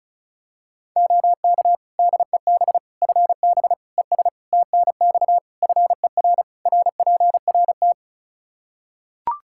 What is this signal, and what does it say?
Morse code, heard as a keyed tone, about 36 wpm, 700 Hz. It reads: OK DEB FB ES TNX FER RPRT
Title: Practice QSO - Exchange 2 - Info - 35wpm